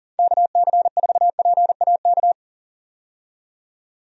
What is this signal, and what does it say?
Morse code, heard as a keyed tone, about 40 wpm, 700 Hz. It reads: KC4PAK
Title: USA Call Signs - 6 Characters - 40wpm